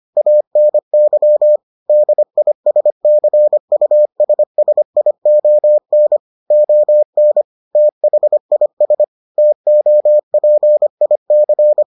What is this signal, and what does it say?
Morse code, heard as a keyed tone, about 25 wpm, 600 Hz. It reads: ANY DISCUSSION ON THIS TOPIC